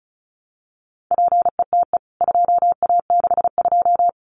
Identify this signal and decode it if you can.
Morse code, heard as a keyed tone, about 34 wpm, 700 Hz. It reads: PETE 2A62
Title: CWT Exchange 35wpm